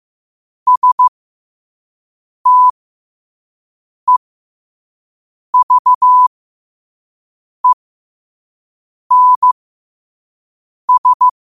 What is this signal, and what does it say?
Morse code, heard as a keyed tone, about 15 wpm, spaced as for 5 wpm, 1000 Hz. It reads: STEVENS
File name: qotdfiles / 5wpm